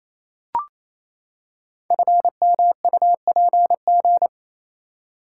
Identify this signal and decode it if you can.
Morse code, heard as a keyed tone, about 29 wpm, 700 Hz. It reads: FMUPG
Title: Code Groups - 5 Letters - 28wpm